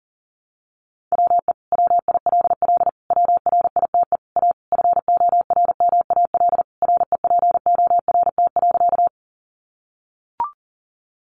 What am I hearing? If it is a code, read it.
Morse code, heard as a keyed tone, about 40 wpm, 700 Hz. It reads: WE WILL WRITE A FORMAL REPORT.
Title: Sentences from Top 1300 Words 40wpm